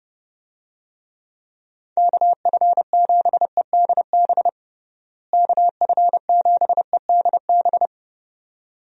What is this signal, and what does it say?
Morse code, heard as a keyed tone, about 32 wpm, 700 Hz. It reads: KF7EDB KF7EDB
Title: USA Call Signs - 6 Characters - Repeated 2x - 30wpm